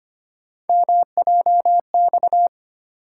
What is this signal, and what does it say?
Morse code, heard as a keyed tone, about 25 wpm, 700 Hz. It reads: MJX
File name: Three Letter-Number Combinations - 25wpm